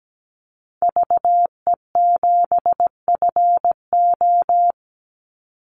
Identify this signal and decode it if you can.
Morse code, heard as a keyed tone, about 17 wpm, 700 Hz. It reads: VE7FO
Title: Call Signs Repeated 3x - 17wpm